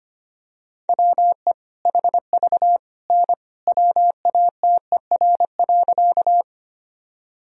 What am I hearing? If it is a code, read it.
Morse code, heard as a keyed tone, about 25 wpm, 700 Hz. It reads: WE HV N WATER.
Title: Sentences from Top 100 Words and Abbreviations 25wpm